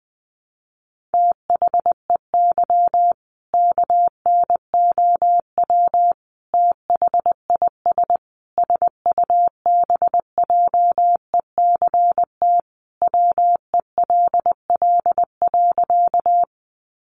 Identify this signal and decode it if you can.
Morse code, heard as a keyed tone, about 20 wpm, 700 Hz. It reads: THEY KNOW THIS SUBJECT WELL.
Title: Sentences from Top 800 Words 20wpm